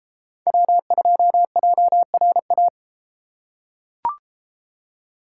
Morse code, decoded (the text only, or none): W2JRA